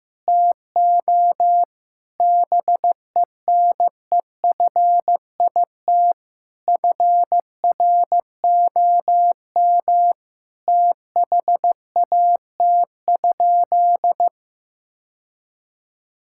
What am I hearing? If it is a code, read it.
Morse code, heard as a keyed tone, about 15 wpm, 700 Hz. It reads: TO BENEFIT FROM THAT?
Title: Sentences from Top 1000 Words - Review Part 6 - 15wpm